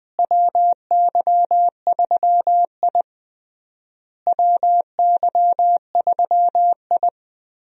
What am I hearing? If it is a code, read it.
Morse code, heard as a keyed tone, about 21 wpm, 700 Hz. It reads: WY3I WY3I
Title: USA Call Signs - 4 Characters - Repeated 3x - 20wpm